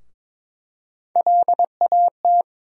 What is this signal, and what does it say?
Morse code, heard as a keyed tone, about 23 wpm, 700 Hz. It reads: LAT
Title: USA Call Sign Postfixes - 3 Letters 22wpm